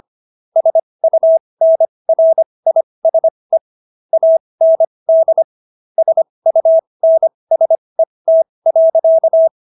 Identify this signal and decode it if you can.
Morse code, heard as a keyed tone, about 25 wpm, spaced as for 20 wpm, 650 Hz. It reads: SUNRISE AND SUNSET.